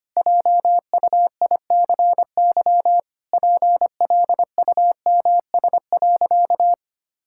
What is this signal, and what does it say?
Morse code, heard as a keyed tone, about 25 wpm, 700 Hz. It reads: JUICY PLUMS.
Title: Pangram Challenge v2 - 25wpm